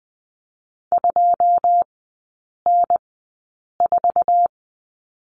Morse code, heard as a keyed tone, about 20 wpm, spaced as for 8 wpm, 700 Hz. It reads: 2N4
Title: 107 Lesson 18e - Letter m - Three Characters 20wpm